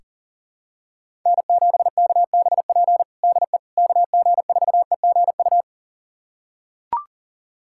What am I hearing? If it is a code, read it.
Morse code, heard as a keyed tone, about 40 wpm, 700 Hz. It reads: N7KBP DE KG4EGU
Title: USA Callsign DE Callsign 40wpm